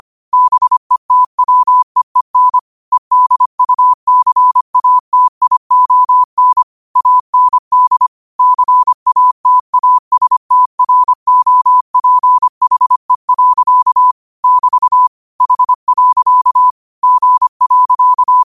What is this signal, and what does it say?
Morse code, heard as a keyed tone, about 25 wpm, 1000 Hz. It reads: BETWEEN EDUCATION AND CATASTROPHE. <BT> H. G.
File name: qotdfiles / 25wpm